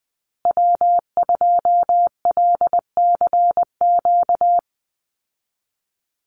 Morse code, heard as a keyed tone, about 20 wpm, 700 Hz. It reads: W2LCQ
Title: Call Signs Repeated 2x - 20wpm